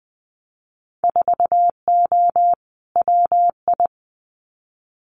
Morse code, heard as a keyed tone, about 20 wpm, 700 Hz. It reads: 4O WI
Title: Winter Field Day Exchanges 20wpm